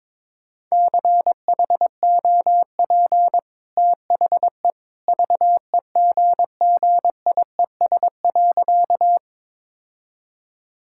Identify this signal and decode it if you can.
Morse code, heard as a keyed tone, about 22 wpm, 690 Hz. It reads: CHOP THE VEGGIES.
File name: practice / 22wpm / Easy Sentences with Hints - 22wpm